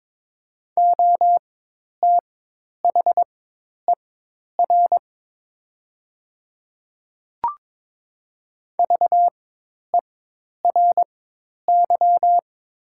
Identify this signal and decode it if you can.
Morse code, heard as a keyed tone, about 21 wpm, spaced as for 10 wpm, 700 Hz. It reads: OTHER VERY
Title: 4x Character Spacing - Top 200 Words - 22wpm